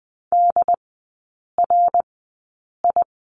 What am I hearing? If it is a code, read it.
Morse code, heard as a keyed tone, about 20 wpm, spaced as for 8 wpm, 700 Hz. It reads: DRI
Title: 093 Lesson 16e - Letter c - Three Characters 20wpm